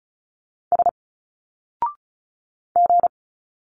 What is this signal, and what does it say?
Morse code, heard as a keyed tone, about 36 wpm, 700 Hz. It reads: S G
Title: Single Letters 35wpm